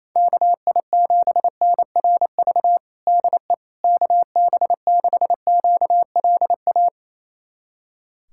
Morse code, heard as a keyed tone, about 27 wpm, 700 Hz. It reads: KI7NRV DE KB6QLA
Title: USA Callsign DE Callsign 28wpm